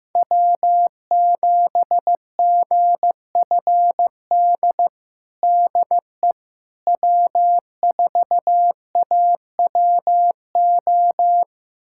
Morse code, heard as a keyed tone, about 15 wpm, 700 Hz. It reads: W7GFD DE W4AWO
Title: USA Callsign DE Callsign - 5 Characters - 15wpm